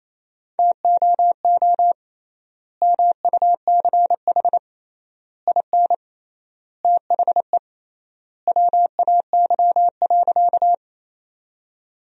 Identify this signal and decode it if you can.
Morse code, heard as a keyed tone, about 27 wpm, 700 Hz. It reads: TOO MUCH IN THE WAY.
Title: 3x Word-Spacing - Sentences from Top 200 Words - 28wpm